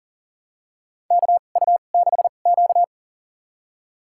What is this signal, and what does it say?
Morse code, heard as a keyed tone, about 40 wpm, spaced as for 29 wpm, 700 Hz. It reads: KU6Q
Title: USA Call Signs - 4 Characters - 2x Character Spacing - 40wpm